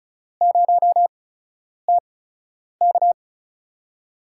Morse code, heard as a keyed tone, about 36 wpm, spaced as for 10 wpm, 700 Hz. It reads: 0TK
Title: USA Call Signs - 5 Characters - 8x Character Spacing - 35wpm